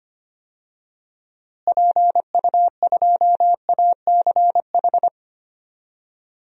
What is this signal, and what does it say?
Morse code, heard as a keyed tone, about 25 wpm, 700 Hz. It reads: PU2ACH
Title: DXCC-100 Call Signs - 6 Characters - Repeated 3x - 25wpm